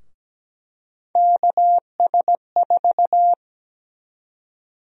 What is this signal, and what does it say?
Morse code, heard as a keyed tone, about 17 wpm, 700 Hz. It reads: KS4
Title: USA Call Sign Prefixes - Repeated 2x - 17wpm